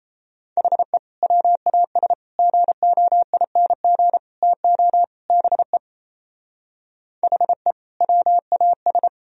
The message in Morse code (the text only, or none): HE WAS GOING TO BE HE WAS